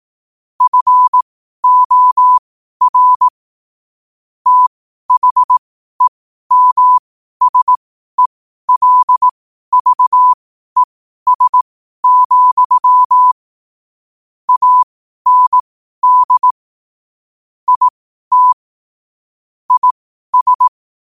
Morse code, heard as a keyed tone, about 18 wpm, spaced as for 12 wpm, 1000 Hz. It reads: FOR THEMSELVES, AND IT IS
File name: qotdfiles / 12wpm